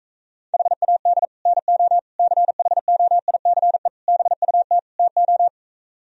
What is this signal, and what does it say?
Morse code, heard as a keyed tone, about 45 wpm, 700 Hz. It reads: HAD NO CHOICE BUT TO